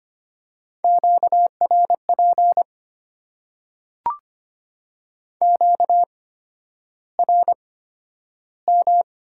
Morse code, heard as a keyed tone, about 25 wpm, spaced as for 11 wpm, 700 Hz. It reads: QRP QRM
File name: Top 90 QSO Elements - 8x Character Spacing - 25wpm